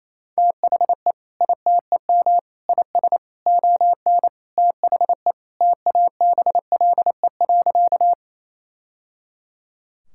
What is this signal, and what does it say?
Morse code, heard as a keyed tone, about 28 wpm, 700 Hz. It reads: THE ITEM IS ON THE TABLE.
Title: Sentences from Top 1000 Words - Review Part 6 - 28wpm